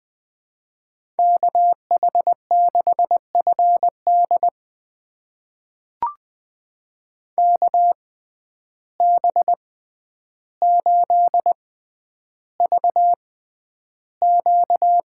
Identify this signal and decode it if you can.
Morse code, heard as a keyed tone, about 20 wpm, spaced as for 10 wpm, 700 Hz. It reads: KH6FD KB8VQ
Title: USA Call Signs - 6x Character Spacing - 20wpm